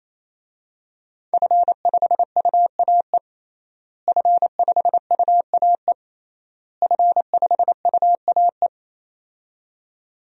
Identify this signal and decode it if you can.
Morse code, heard as a keyed tone, about 28 wpm, 700 Hz. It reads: F5UAE F5UAE F5UAE
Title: DXCC-100 Call Signs - 5 Characters - Repeated 3x - 28wpm